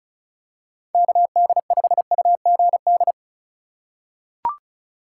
Morse code, heard as a keyed tone, about 34 wpm, 700 Hz. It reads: KD5UGD
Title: USA Call Signs - 6 Characters - 3x Character Spacing - 35wpm